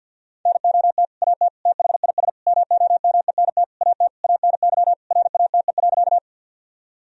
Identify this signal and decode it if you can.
Morse code, heard as a keyed tone, about 51 wpm, 700 Hz. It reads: NOT AT THIS MOMENT AT ANY RATE.
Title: Sentences from Top 1000 Words - Review Part 4 - 50wpm